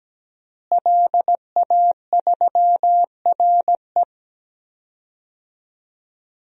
Morse code, heard as a keyed tone, about 17 wpm, 700 Hz. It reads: LA3RE
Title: DXCC-100 Call Signs - 5 Characters - Repeated 2x - 17wpm